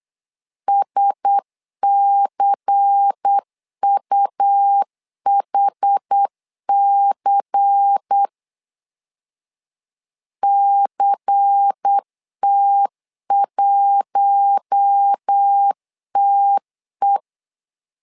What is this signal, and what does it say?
Morse code, heard as a keyed tone, about 8 wpm, 800 Hz. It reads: SCUHC CT1TE
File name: LEZIONI / 5 LBVC / 5 LBVC 8wpm 2935